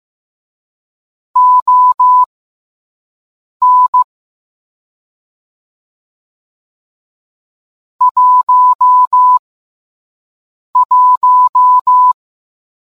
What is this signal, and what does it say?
Morse code, heard as a keyed tone, about 15 wpm, spaced as for 5 wpm, 1000 Hz. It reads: ON 11